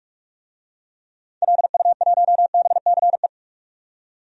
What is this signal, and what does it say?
Morse code, heard as a keyed tone, about 46 wpm, 700 Hz. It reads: LU1BCE
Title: DXCC-100 Call Signs - 6 Characters - 45wpm